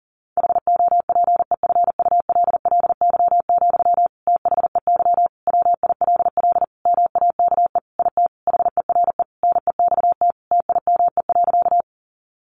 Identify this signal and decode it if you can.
Morse code, heard as a keyed tone, about 40 wpm, 700 Hz. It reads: HOPEFULLY, THEY WILL MAKE IT HERE NEXT TIME.
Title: Sentences from Top 1500 Words 40wpm